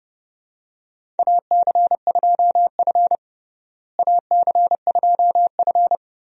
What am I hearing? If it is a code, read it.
Morse code, heard as a keyed tone, about 29 wpm, 700 Hz. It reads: AC2F AC2F
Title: USA Call Signs - 4 Characters - Repeated 3x - 30wpm